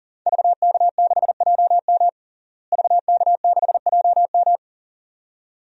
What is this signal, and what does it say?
Morse code, heard as a keyed tone, about 40 wpm, 700 Hz. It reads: VK6JM VK6JM
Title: DXCC-100 Call Signs - 5 Characters - Repeated 2x - 40wpm